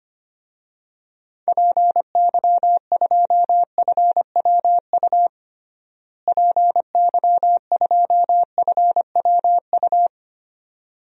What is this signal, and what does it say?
Morse code, heard as a keyed tone, about 25 wpm, 700 Hz. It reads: PY2FWU PY2FWU
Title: DXCC-100 Call Signs - 6 Characters - Repeated 2x - 25wpm